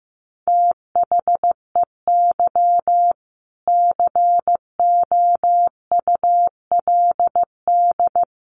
Morse code, heard as a keyed tone, about 15 wpm, 700 Hz. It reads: THEY COULD
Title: Sentences from Top 1400 Words 15wpm